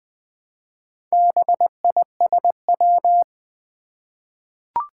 Morse code, heard as a keyed tone, about 20 wpm, 700 Hz. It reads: BISW
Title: Four Challenging Letter-Number Combinations 20wpm